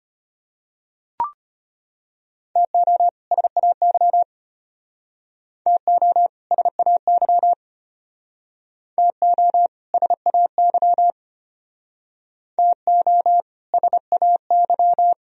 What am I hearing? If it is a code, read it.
Morse code, heard as a keyed tone, about 34 wpm, 700 Hz. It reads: TO SAY TO SAY TO SAY TO SAY